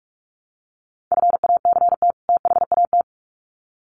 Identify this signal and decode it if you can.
Morse code, heard as a keyed tone, about 44 wpm, 700 Hz. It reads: FACT THAT
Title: Sets of 2 Words 45wpm